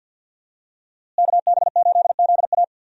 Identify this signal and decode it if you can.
Morse code, heard as a keyed tone, about 51 wpm, 700 Hz. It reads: KB8ZA
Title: USA Call Signs - 5 Characters - Repeated 3x - 50wpm